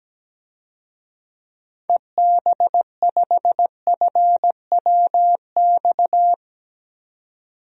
Morse code, heard as a keyed tone, about 17 wpm, 700 Hz. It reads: EB5FWX
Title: DXCC-100 Call Signs - 6 Characters - 17wpm